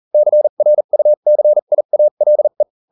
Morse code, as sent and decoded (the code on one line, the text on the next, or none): -.-. .-. ..- -.-. .. .- .-.. .
CRUCIALE